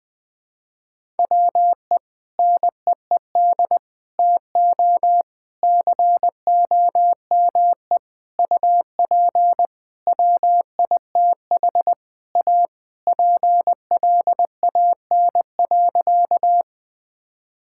Morse code, heard as a keyed tone, about 20 wpm, 700 Hz. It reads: WE NEED TO COME UP WITH A PLAN.
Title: Sentences with 5-Word Phrases - 20wpm